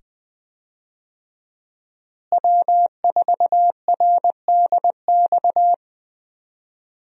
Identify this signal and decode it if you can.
Morse code, heard as a keyed tone, about 20 wpm, 700 Hz. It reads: W4RDX
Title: USA Call Signs - 5 Characters - Repeated 2x - 20wpm